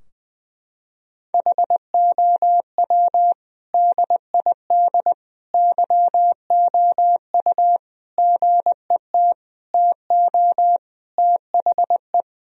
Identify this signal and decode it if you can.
Morse code, heard as a keyed tone, about 21 wpm, 700 Hz. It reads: HOW DID YOU GET TO THE
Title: Sentences from Top 1000 Words - Review Part 3 - 20wpm